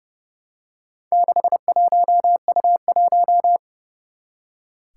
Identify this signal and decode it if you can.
Morse code, heard as a keyed tone, about 28 wpm, 700 Hz. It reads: 61U1
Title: Four Challenging Letter-Number Combinations 30wpm